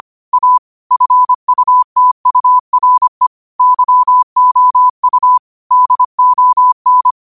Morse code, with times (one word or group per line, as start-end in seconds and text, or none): 0.3-0.6 A
0.9-3.3 FUTURE
3.6-5.4 YOU
5.7-7.1 DON